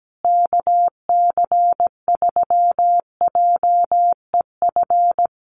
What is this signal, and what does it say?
Morse code, heard as a keyed tone, about 17 wpm, 700 Hz. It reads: KC3JEF
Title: USA Call Signs Repeated 3x 17wpm